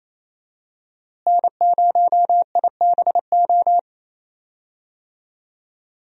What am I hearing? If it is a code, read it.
Morse code, heard as a keyed tone, about 28 wpm, 700 Hz. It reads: N0IBO
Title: USA Call Signs - 5 Characters - Repeated 3x - 28wpm